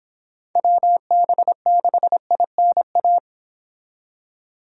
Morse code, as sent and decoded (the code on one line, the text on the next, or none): .-- -... -.... .. -. .-
WB6INA